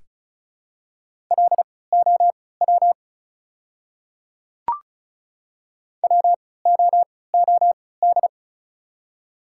Morse code, heard as a keyed tone, about 34 wpm, spaced as for 20 wpm, 700 Hz. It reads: LOW WOOD
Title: 3x Character Spacing - Top 300 Words - 35wpm